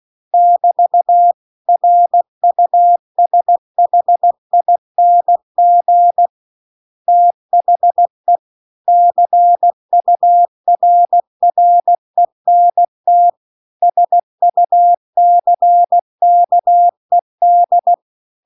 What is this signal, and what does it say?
Morse code, heard as a keyed tone, about 16 wpm, 700 Hz. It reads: <BT> RUSHING THE CURRENT SUCKED